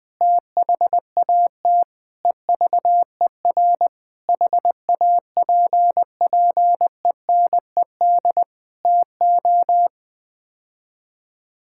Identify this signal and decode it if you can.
Morse code, heard as a keyed tone, about 21 wpm, 700 Hz. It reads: THAT EVER HAPPENED TO
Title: Sets of 8 Words - 20wpm